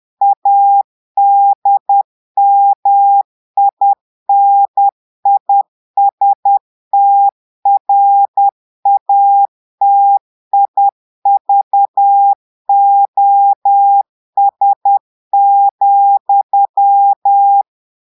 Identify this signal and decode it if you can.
Morse code, heard as a keyed tone, about 10 wpm, 800 Hz. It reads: ADMINISTRATIVOS,
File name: portugais texte en clair 2 vitesse 10 mots minutes